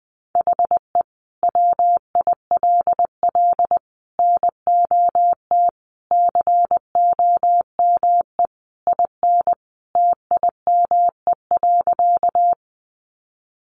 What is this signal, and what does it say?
Morse code, heard as a keyed tone, about 20 wpm, 700 Hz. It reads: HE WILL NOT COME IN TIME.
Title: Sentences from Top 600 Words 20wpm